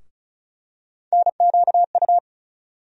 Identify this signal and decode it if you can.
Morse code, heard as a keyed tone, about 34 wpm, 700 Hz. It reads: NQU
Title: USA Call Sign Postfixes - 3 Letters 35wpm